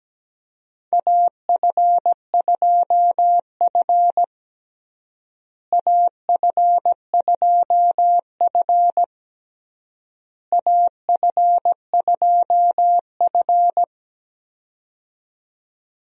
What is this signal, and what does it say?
Morse code, heard as a keyed tone, about 17 wpm, 700 Hz. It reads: AF2F AF2F AF2F
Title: USA Call Signs - 4 Characters - Repeated 3x - 17wpm